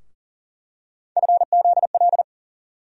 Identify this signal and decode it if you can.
Morse code, heard as a keyed tone, about 40 wpm, 700 Hz. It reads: FZL
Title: USA Call Sign Postfixes - 3 Letters 40wpm